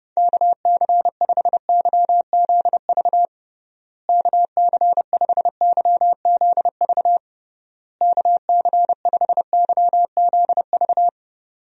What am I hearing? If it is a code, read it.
Morse code, heard as a keyed tone, about 30 wpm, 700 Hz. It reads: KC5YZV KC5YZV KC5YZV
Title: USA Call Signs - 6 Characters - Repeated 3x - 30wpm